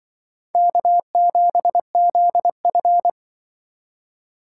Morse code, heard as a keyed tone, about 23 wpm, 700 Hz. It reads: K7ZF
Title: Speed Racing - USA 1x2 Call Signs 20wpm